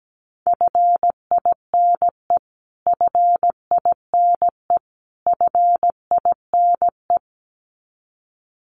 Morse code, heard as a keyed tone, about 17 wpm, 700 Hz. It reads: FINE FINE FINE
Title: Top 500 Words Repeated 17wpm